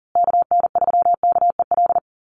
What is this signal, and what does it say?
Morse code, heard as a keyed tone, about 40 wpm, 700 Hz. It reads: KN3KEL
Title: USA Call Signs Repeated 3x 40wpm